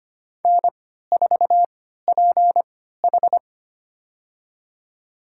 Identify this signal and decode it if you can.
Morse code, heard as a keyed tone, about 25 wpm, spaced as for 14 wpm, 700 Hz. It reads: N4PH
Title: USA Call Signs - 4 Characters - 3x Character Spacing - 25wpm